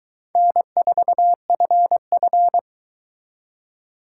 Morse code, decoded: N4FF

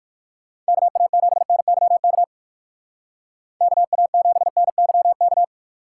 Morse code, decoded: KA7NYK KA7NYK

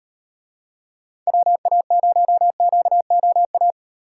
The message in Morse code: WA0QOA

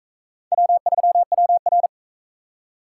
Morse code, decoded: W3WR

